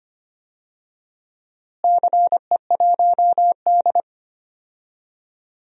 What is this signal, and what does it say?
Morse code, heard as a keyed tone, about 25 wpm, 700 Hz. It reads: CE1D
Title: DXCC-100 Call Signs - 4 Characters - Repeated 2x - 25wpm